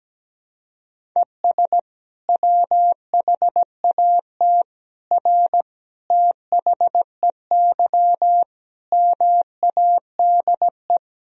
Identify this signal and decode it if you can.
Morse code, heard as a keyed tone, about 17 wpm, 700 Hz. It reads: ES WHAT R THEY MADE
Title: Sentences from Top 200 Words and Abbreviations 17wpm